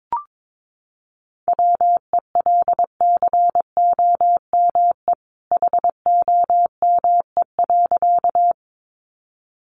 Morse code, heard as a keyed tone, about 22 wpm, 700 Hz. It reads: WELCOME HOME.
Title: A Sound Of Thunder 22wpm